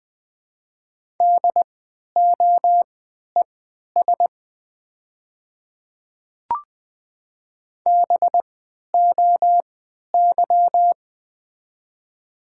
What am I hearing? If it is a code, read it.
Morse code, heard as a keyed tone, about 20 wpm, spaced as for 11 wpm, 700 Hz. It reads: DOES BOY
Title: 3x Character Spacing - Top 200 Words - 20wpm